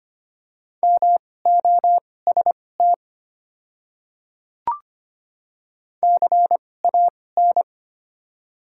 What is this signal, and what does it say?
Morse code, heard as a keyed tone, about 25 wpm, spaced as for 18 wpm, 700 Hz. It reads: MOST CAN
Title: 2x Character Spacing - Top 100 Words - 25wpm